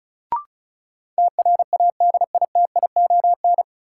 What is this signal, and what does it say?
Morse code, heard as a keyed tone, about 36 wpm, 700 Hz. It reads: TRADITION
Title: Sentences from Top 1500 Words - New Word Drill - 35wpm